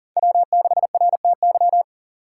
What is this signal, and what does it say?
Morse code, heard as a keyed tone, about 39 wpm, 700 Hz. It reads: W6RTY
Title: USA Call Signs - 5 Characters - 4x Character Spacing - 40wpm